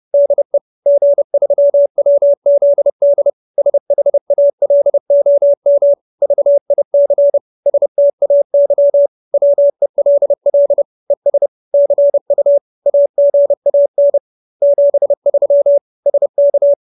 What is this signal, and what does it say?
Morse code, heard as a keyed tone, about 30 wpm, 570 Hz. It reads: DE G3WZD SHALOM VIC STAY WELL ES CU AGAN 73 SK
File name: QSO310 30